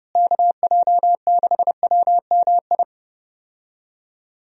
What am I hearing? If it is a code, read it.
Morse code, heard as a keyed tone, about 30 wpm, 700 Hz. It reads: KJ6WMI
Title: USA Call Signs - 6 Characters - Repeated 3x - 30wpm